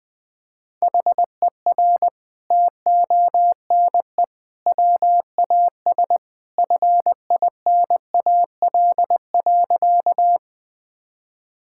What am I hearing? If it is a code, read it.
Morse code, heard as a keyed tone, about 20 wpm, 700 Hz. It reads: HER TONE WAS FINAL.
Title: Sentences from Top 1600 Words 20wpm